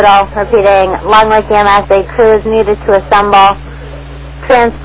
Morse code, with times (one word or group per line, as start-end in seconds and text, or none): none